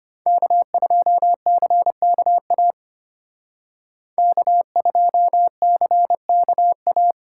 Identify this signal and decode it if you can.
Morse code, heard as a keyed tone, about 29 wpm, 700 Hz. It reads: K2CKA K2CKA
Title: Speed Racing - USA Call Signs 25wpm